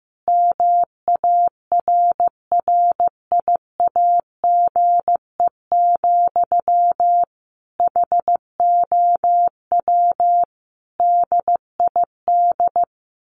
Morse code, heard as a keyed tone, about 15 wpm, 700 Hz. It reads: MARRIAGE, HOW DID
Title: Sentences from Top 1100 Words 15wpm